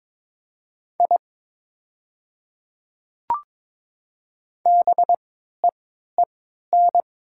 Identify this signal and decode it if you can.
Morse code, heard as a keyed tone, about 22 wpm, spaced as for 13 wpm, 700 Hz. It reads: I BEEN
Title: 3x Character Spacing - Top 100 Words - 22wpm